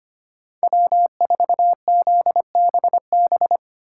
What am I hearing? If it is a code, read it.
Morse code, heard as a keyed tone, about 33 wpm, 700 Hz. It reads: W4ZBB